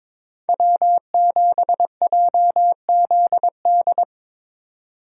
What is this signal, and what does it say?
Morse code, heard as a keyed tone, about 22 wpm, 700 Hz. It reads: W7JZD
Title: USA Call Signs - 5 Characters - 22wpm